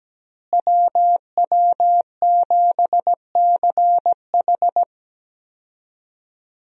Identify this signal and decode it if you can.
Morse code, heard as a keyed tone, about 17 wpm, 700 Hz. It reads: WW7CH
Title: USA Call Signs - 5 Characters - Repeated 2x - 17wpm